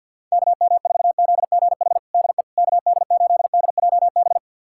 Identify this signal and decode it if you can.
Morse code, heard as a keyed tone, about 49 wpm, 700 Hz. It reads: KM4ZGH DE KD8DJB
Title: USA Callsign DE Callsign 50wpm